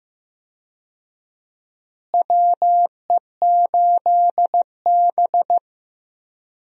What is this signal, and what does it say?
Morse code, heard as a keyed tone, about 15 wpm, 700 Hz. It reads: WE8B
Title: USA Call Signs - 4 Characters - Repeated 2x - 15wpm